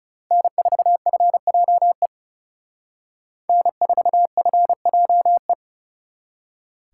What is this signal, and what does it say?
Morse code, heard as a keyed tone, about 39 wpm, 700 Hz. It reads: N4FJE N4FJE